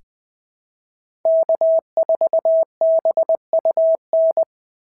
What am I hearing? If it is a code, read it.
Morse code, heard as a keyed tone, about 20 wpm, 690 Hz. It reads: K4BUN
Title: SOTA QSO Practice 20wpm